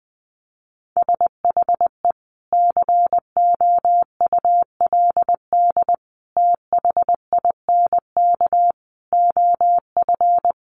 Morse code, heard as a keyed tone, about 20 wpm, 700 Hz. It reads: SHE COULD THINK OF
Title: Sentences from Top 500 Words 20wpm